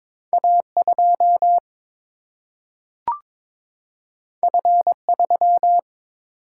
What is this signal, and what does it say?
Morse code, heard as a keyed tone, about 22 wpm, 700 Hz. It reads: A 2 F 3
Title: Single-Letter Single-Number 22wpm